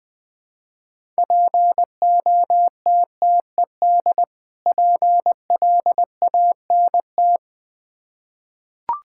Morse code, heard as a keyed tone, about 21 wpm, 700 Hz. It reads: POTTED PLANT
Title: Difficult Words with Hints - 20wpm